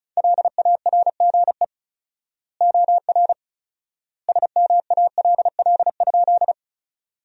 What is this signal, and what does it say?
Morse code, heard as a keyed tone, about 36 wpm, 700 Hz. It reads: LARGE OR SMALL?
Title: 4x Word-Spacing - Sentences from Top 200 Words - 35wpm